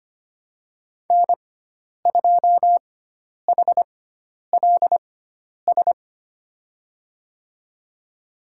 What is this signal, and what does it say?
Morse code, heard as a keyed tone, about 25 wpm, spaced as for 10 wpm, 700 Hz. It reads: N2HLS
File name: Lesson 16g - Letter c - Call Signs 25wpm